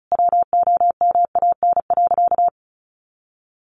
Morse code, heard as a keyed tone, about 34 wpm, 700 Hz. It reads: WOMAN.
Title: Sentences from Top 1100 Words 35wpm